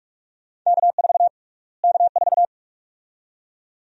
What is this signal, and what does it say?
Morse code, heard as a keyed tone, about 42 wpm, 700 Hz. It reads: K4 K4
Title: USA Call Sign Prefixes - Repeated 2x - 45wpm